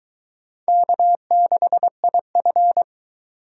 Morse code, heard as a keyed tone, about 22 wpm, 700 Hz. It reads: K6IF